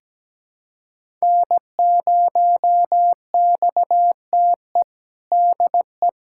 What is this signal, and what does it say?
Morse code, heard as a keyed tone, about 17 wpm, 700 Hz. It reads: N0XTE DE
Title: USA Callsign DE Callsign 17wpm